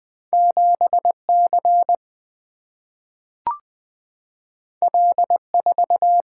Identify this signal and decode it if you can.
Morse code, heard as a keyed tone, about 20 wpm, 700 Hz. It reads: 7C L4
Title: Two Letter-Number Combinations - 20wpm